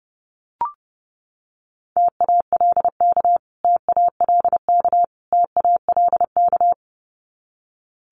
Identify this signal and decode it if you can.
Morse code, heard as a keyed tone, about 30 wpm, 700 Hz. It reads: TALK TALK TALK
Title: Top 500 Words Repeated 30wpm